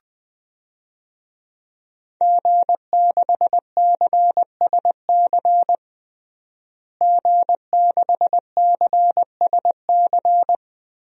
Morse code, heard as a keyed tone, about 21 wpm, 700 Hz. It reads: G6CSC G6CSC
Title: DXCC-100 Call Signs - 5 Characters - Repeated 3x - 20wpm